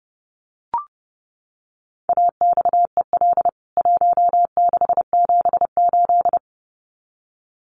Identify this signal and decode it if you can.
Morse code, heard as a keyed tone, about 30 wpm, 700 Hz. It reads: AXEL 1678
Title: CWT Exchange 30wpm